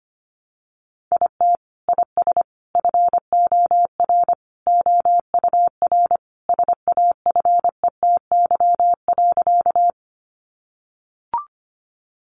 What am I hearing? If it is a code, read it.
Morse code, heard as a keyed tone, about 25 wpm, 700 Hz. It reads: IT IS FOR OUR SAFETY.
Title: Sentences from Top 700 Words 25wpm